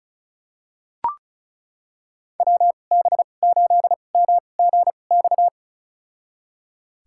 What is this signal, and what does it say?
Morse code, heard as a keyed tone, about 34 wpm, spaced as for 25 wpm, 700 Hz. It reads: WB8MGX
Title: USA Call Signs - 6 Characters - 2x Character Spacing - 35wpm